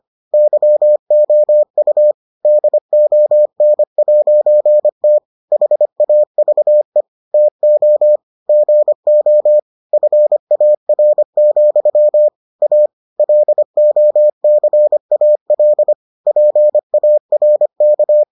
Morse code, heard as a keyed tone, about 25 wpm, 600 Hz. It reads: YOU DON'T HAVE TO GO FAR, A LOCAL PARK